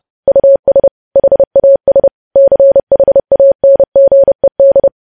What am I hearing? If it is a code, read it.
Morse code, heard as a keyed tone, about 30 wpm, 550 Hz. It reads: US HAS CHANGED
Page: news headlines in morse code at 30 wpm